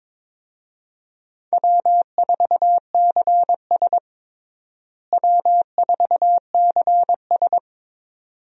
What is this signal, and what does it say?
Morse code, heard as a keyed tone, about 22 wpm, 700 Hz. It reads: W4CS W4CS
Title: USA Call Signs - 4 Characters - Repeated 2x - 22wpm